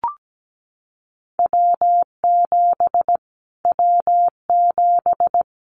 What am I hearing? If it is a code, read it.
Morse code, heard as a keyed tone, about 17 wpm, 700 Hz. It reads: W7 W7
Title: Callsign Prefixes for Top 25 DXCC - Repeated 3x - 17wpm